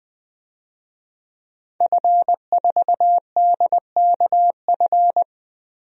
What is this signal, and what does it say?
Morse code, heard as a keyed tone, about 20 wpm, 700 Hz. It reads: F4DKF
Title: DXCC-100 Call Signs - 5 Characters - Repeated 2x - 20wpm